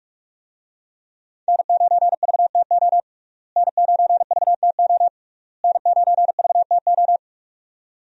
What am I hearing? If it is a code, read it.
Morse code, heard as a keyed tone, about 45 wpm, 700 Hz. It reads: N9VTO N9VTO N9VTO
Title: USA Call Signs - 5 Characters - Repeated 3x - 45wpm